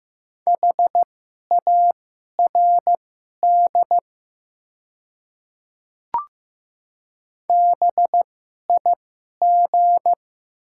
Morse code, heard as a keyed tone, about 15 wpm, spaced as for 11 wpm, 700 Hz. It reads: HARD BIG
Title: 2x Character Spacing - Top 300 Words - 15wpm